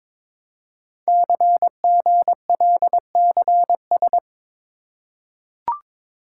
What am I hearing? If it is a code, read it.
Morse code, heard as a keyed tone, about 22 wpm, 700 Hz. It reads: CGLCS
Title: Code Groups - 5 Letters - 22wpm